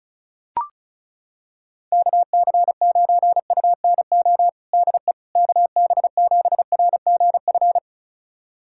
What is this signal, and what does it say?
Morse code, heard as a keyed tone, about 34 wpm, 700 Hz. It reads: KC9UNO DE KB7RGF
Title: USA Callsign DE Callsign 35wpm